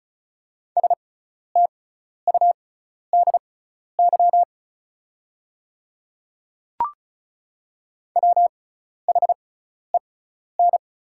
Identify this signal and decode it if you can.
Morse code, heard as a keyed tone, about 36 wpm, spaced as for 12 wpm, 690 Hz. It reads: STUDY WHEN
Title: 6x Character Spacing - Top 300 Words - 35wpm